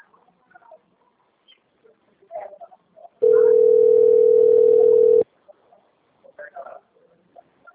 Ringback tone (North America), 440 + 480 Hz.
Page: Dourados - MS